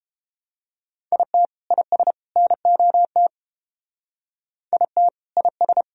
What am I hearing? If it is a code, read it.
Morse code, heard as a keyed tone, about 28 wpm, 700 Hz. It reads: IT IS NOT IT IS